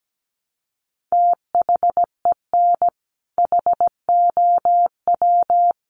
Morse code, heard as a keyed tone, about 17 wpm, 700 Hz. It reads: THEN HOW
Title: Sentences from Top 1400 Words 17wpm